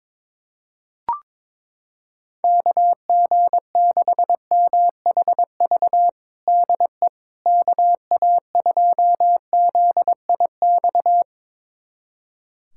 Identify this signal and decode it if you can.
Morse code, heard as a keyed tone, about 22 wpm, 700 Hz. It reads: KG6MHV DE KA2ZIX
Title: USA Callsign DE Callsign 22wpm